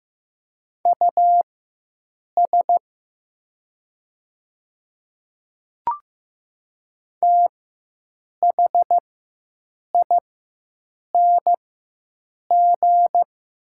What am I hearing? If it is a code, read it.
Morse code, heard as a keyed tone, about 15 wpm, spaced as for 7 wpm, 700 Hz. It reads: US THING
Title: 4x Character Spacing - Top 500 Words - 15wpm